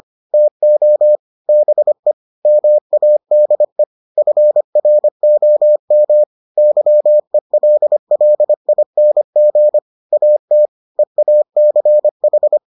Morse code, heard as a keyed tone, about 25 wpm, 600 Hz. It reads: TO BE MADE FROM YELLING AT EACH